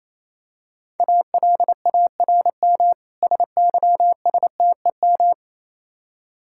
Morse code, heard as a keyed tone, about 27 wpm, 700 Hz. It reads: ALARM SYSTEM
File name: Easy Words - Speed-Racing with Hints - 22wpm